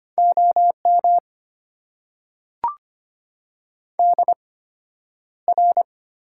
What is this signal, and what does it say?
Morse code, heard as a keyed tone, about 25 wpm, spaced as for 11 wpm, 690 Hz. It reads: OM DR
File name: Top 90 QSO Elements - 8x Character Spacing - 25wpm